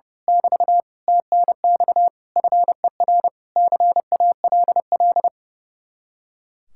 Morse code, heard as a keyed tone, about 30 wpm, 700 Hz. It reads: <BT> TNX FER CALL
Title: Speed-Racing Common QSO Phrases - 30wpm